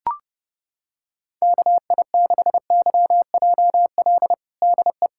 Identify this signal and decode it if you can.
Morse code, heard as a keyed tone, about 32 wpm, 700 Hz. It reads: KI6YJL DE
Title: USA Callsign DE Callsign - 6 Characters - 30wpm